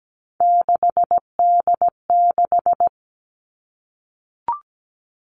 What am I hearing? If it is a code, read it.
Morse code, heard as a keyed tone, about 17 wpm, 700 Hz. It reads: 6D6
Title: Three Challenging Letter-Number Combinations 17wpm